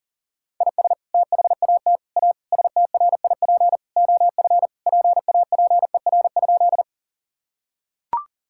Morse code, heard as a keyed tone, about 40 wpm, 700 Hz. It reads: IS THAT A STRIP OF PAPER?
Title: Sentences from Top 1600 Words 40wpm